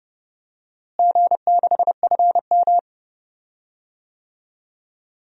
Morse code, heard as a keyed tone, about 30 wpm, 700 Hz. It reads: G6FM
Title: DXCC-100 Call Signs - 4 Characters - Repeated 2x - 30wpm